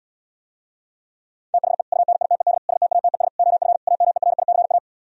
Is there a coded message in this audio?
Yes: Morse code from 1.5 to 4.8 s, heard as a keyed tone, about 77 wpm, 700 Hz. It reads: THE UNITED STATES OF AMERICA